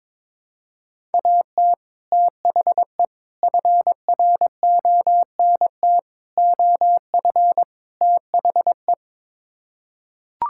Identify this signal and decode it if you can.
Morse code, heard as a keyed tone, about 22 wpm, 700 Hz. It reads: AT THE FRONT OF THE
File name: practice / 22wpm / Build A Phrase - 5-Word Phrases - 22wpm